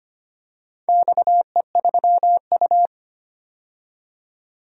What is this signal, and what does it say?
Morse code, heard as a keyed tone, about 25 wpm, 700 Hz. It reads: XE3U
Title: DXCC-100 Call Signs - 4 Characters - 25wpm